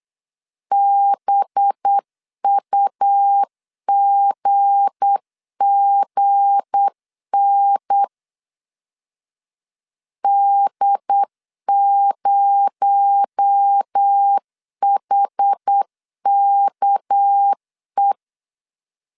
Morse code, heard as a keyed tone, about 8 wpm, 800 Hz. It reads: BUGGN D0HKE
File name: LEZIONI / 7 GF9Z / 7 G 8wpm 3118